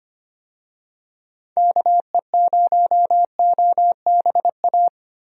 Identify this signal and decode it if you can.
Morse code, heard as a keyed tone, about 25 wpm, 700 Hz. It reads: KE0OBA
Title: USA Call Signs - 6 Characters - Repeated 2x - 25wpm